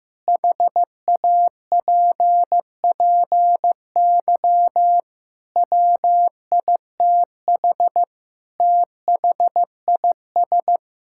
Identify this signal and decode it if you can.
Morse code, heard as a keyed tone, about 16 wpm, 700 Hz. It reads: HAPPY WITH THIS
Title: Sentences from Top 1000 Words - Review Part 5 - 15wpm